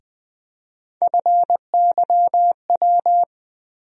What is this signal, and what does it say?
Morse code, heard as a keyed tone, about 20 wpm, 700 Hz. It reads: FYW
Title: USA Call Sign Postfixes - 3 Letters 20wpm